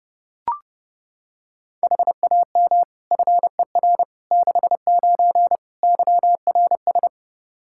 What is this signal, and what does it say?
Morse code, heard as a keyed tone, about 32 wpm, 700 Hz. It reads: HAM FER 69 YRS
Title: K1USN SST QSO Practice 30wpm